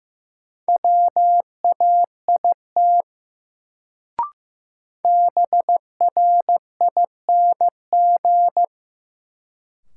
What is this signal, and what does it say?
Morse code, heard as a keyed tone, about 15 wpm, 700 Hz. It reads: WAIT BRING
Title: Sets of 5 Words Encore - New Word Drill - 15wpm